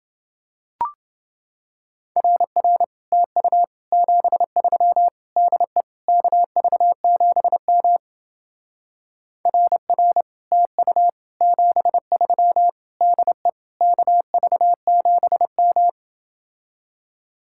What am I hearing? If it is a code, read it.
Morse code, heard as a keyed tone, about 30 wpm, 700 Hz. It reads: RR TU 73 DE KV7M RR TU 73 DE KV7M